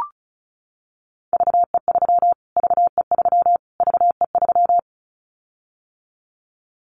Morse code, heard as a keyed tone, about 34 wpm, 700 Hz. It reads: VE3 VE3 VE3
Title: Callsign Prefixes for Top 25 DXCC - Repeated 3x - 35wpm